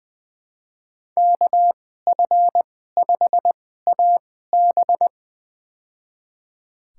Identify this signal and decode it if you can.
Morse code, heard as a keyed tone, about 21 wpm, spaced as for 15 wpm, 700 Hz. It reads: KF5AB
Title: USA Call Signs - 2x Character Spacing - 20wpm